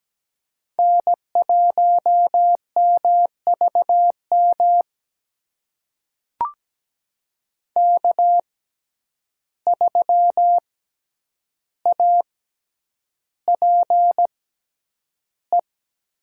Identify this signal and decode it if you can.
Morse code, heard as a keyed tone, about 17 wpm, spaced as for 9 wpm, 700 Hz. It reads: N1MVM K3APE
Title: USA Call Signs - 5 Characters - 6x Character Spacing - 17wpm